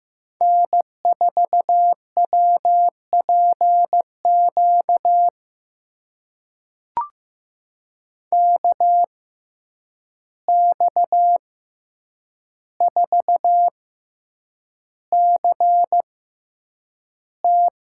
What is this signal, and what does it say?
Morse code, heard as a keyed tone, about 15 wpm, spaced as for 8 wpm, 700 Hz. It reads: N4WPQ KX4CT
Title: USA Call Signs - 5 Characters - 6x Character Spacing - 15wpm